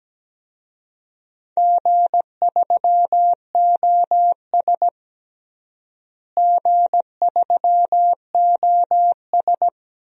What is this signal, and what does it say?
Morse code, heard as a keyed tone, about 17 wpm, 700 Hz. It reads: G3OS G3OS
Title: DXCC-100 Call Signs - 4 Characters - Repeated 2x - 17wpm